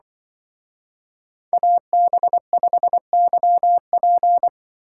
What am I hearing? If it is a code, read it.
Morse code, heard as a keyed tone, about 24 wpm, 700 Hz. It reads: AB5YP
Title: Speed Racing - USA 2x2 Call Signs 20wpm